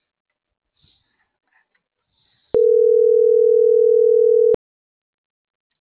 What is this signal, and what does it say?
Ringback tone (North America), 440 + 480 Hz.